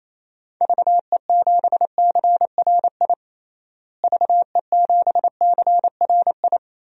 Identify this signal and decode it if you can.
Morse code, heard as a keyed tone, about 28 wpm, 700 Hz. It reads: VE7CRI VE7CRI
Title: DXCC-100 Call Signs - 6 Characters - Repeated 3x - 28wpm